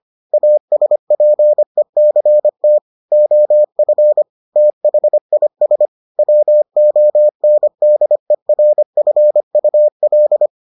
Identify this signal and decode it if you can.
Morse code, heard as a keyed tone, about 25 wpm, 600 Hz. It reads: ASPECT OF THIS WONDERFUL